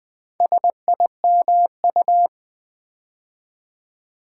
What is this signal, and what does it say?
Morse code, heard as a keyed tone, about 20 wpm, 700 Hz. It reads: SIMU
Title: Build A Word - 4 Letters At A Time - 8-Letter Words - 20wpm